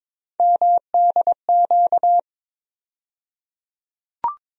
Morse code, heard as a keyed tone, about 21 wpm, 700 Hz. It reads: MDQ
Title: USA Call Sign Postfixes - 3 Letters 22wpm